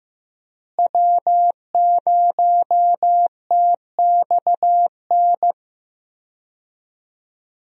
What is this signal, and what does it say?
Morse code, heard as a keyed tone, about 15 wpm, 700 Hz. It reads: W0TXN
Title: USA Call Signs - 5 Characters - Repeated 3x - 15wpm